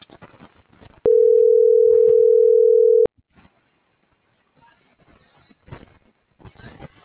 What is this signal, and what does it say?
Ringback tone (North America), 440 + 480 Hz.